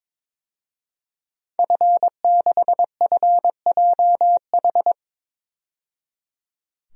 Morse code, heard as a keyed tone, about 22 wpm, 700 Hz. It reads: F6FJH
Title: DXCC-100 Call Signs - 5 Characters - 22wpm